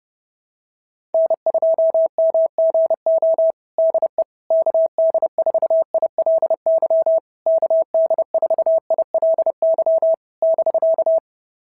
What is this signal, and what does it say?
Morse code, heard as a keyed tone, about 30 wpm, 690 Hz. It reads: N2MGO DE KD4ILY KD4ILY <BK>
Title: SOTA QSO Practice 30wpm